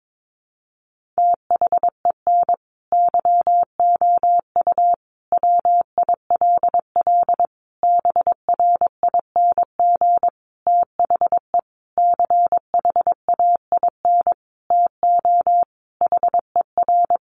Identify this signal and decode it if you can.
Morse code, heard as a keyed tone, about 22 wpm, 700 Hz. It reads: THEN YOU WILL BRING THE CHAIN TO HER
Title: Sentences from Top 1500 Words 22wpm